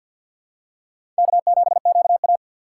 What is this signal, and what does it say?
Morse code, heard as a keyed tone, about 48 wpm, 700 Hz. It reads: K7QA
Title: USA Call Signs - 4 Characters - 2x Character Spacing - 50wpm